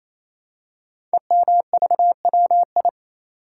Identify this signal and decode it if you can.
Morse code, heard as a keyed tone, about 27 wpm, 700 Hz. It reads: EMVWI
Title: Code Groups - 5 Letters - 28wpm